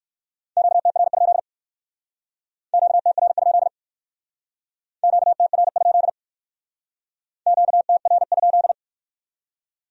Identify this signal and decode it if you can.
Morse code, heard as a keyed tone, about 58 wpm, 700 Hz. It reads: QTR? QTR? QTR? QTR?